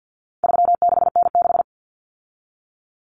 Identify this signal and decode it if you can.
Morse code, heard as a keyed tone, about 50 wpm, 700 Hz. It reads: 36N6
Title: Common Cut Numbers - 4-Digit 50wpm